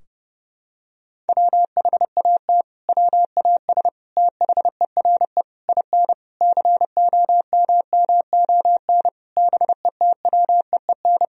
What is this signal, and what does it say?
Morse code, heard as a keyed tone, about 30 wpm, 700 Hz. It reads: WHAT WAS THERE IN COMMON BETWEEN
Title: Sentences from Top 1000 Words - Review Part 3 - 30wpm